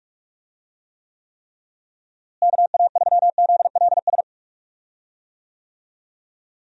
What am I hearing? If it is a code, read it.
Morse code, heard as a keyed tone, about 44 wpm, 700 Hz. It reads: KA3ZLS
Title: USA Call Signs - 6 Characters - Repeated 3x - 45wpm